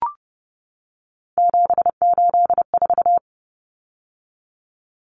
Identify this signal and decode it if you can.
Morse code, heard as a keyed tone, about 30 wpm, 700 Hz. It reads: 784
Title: Common Cut Numbers - 3-Digit 30wpm